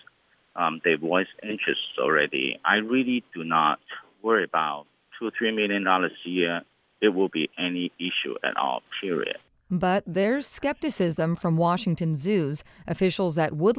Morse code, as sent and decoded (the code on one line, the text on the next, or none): none